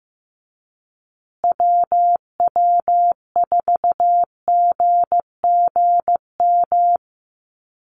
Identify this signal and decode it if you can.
Morse code, heard as a keyed tone, about 15 wpm, 700 Hz. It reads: WW4GGM
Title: USA Call Signs Repeated 2x 15wpm